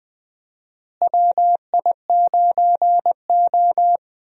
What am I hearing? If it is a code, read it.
Morse code, heard as a keyed tone, about 20 wpm, 700 Hz. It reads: WI9O
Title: USA Call Signs - 4 Characters - Repeated 2x - 20wpm